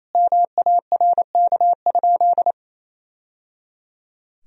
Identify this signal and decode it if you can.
Morse code, heard as a keyed tone, about 28 wpm, 700 Hz. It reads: MARK?
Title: Sentences from Top 1000 Words - Review Part 1 - 28wpm